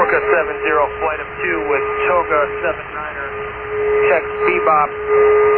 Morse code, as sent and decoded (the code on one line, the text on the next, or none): none